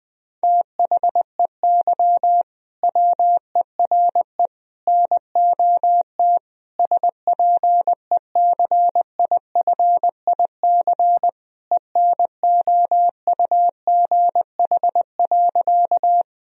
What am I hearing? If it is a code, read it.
Morse code, heard as a keyed tone, about 20 wpm, 700 Hz. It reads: THEY WERE NOT SPECIFIC ENOUGH.
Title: Sentences from Top 1000 Words - Review Part 3 - 20wpm